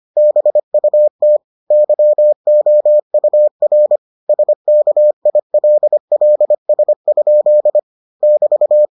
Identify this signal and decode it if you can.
Morse code, heard as a keyed tone, about 25 wpm, 600 Hz. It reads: BUT YOUR SKILLS? <BT>